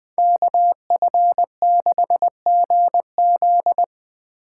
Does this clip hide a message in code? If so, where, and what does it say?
Yes: Morse code from 0.2 to 3.8 s, heard as a keyed tone, about 21 wpm, 700 Hz. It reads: KF6GZ